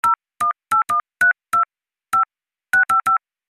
DTMF, heard as telephone touch tones, tones of about 120 ms. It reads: 0181 25655
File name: Sound FX/phone dialled